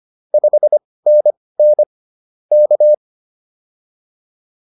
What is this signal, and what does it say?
Morse code, heard as a keyed tone, about 25 wpm, spaced as for 18 wpm, 600 Hz. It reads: 5NN K
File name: POTA101 18